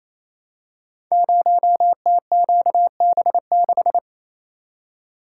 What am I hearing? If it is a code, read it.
Morse code, heard as a keyed tone, about 28 wpm, 700 Hz. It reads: 0TQB6
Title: Code Groups - 5 Letter:Numbers - 28wpm